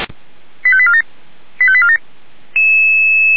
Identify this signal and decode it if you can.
Five-tone selective call (ZVEI), heard as quick runs of five tones, tones of about 70 ms. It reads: 85847 85847
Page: /464/SDR/ZVEI/